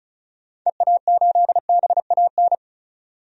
Morse code, heard as a keyed tone, about 34 wpm, 700 Hz. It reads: EA8BAN
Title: DXCC-100 Call Signs - 6 Characters - 35wpm